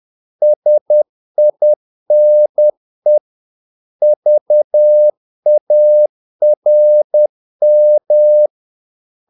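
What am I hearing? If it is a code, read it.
Morse code, heard as a keyed tone, about 10 wpm, 600 Hz. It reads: SINE VARM
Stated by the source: Morse practice of short Danish words at 10 wpm, from CW med Gnister.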